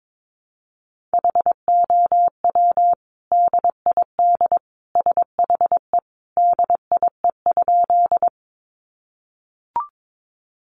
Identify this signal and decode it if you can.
Morse code, heard as a keyed tone, about 22 wpm, 700 Hz. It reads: HOW DID SHE DIE?
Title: Sentences from Top 1300 Words 22wpm